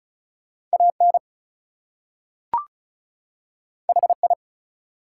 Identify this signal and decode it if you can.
Morse code, heard as a keyed tone, about 36 wpm, 700 Hz. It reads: AN HI
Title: Two Letter Words - 35wpm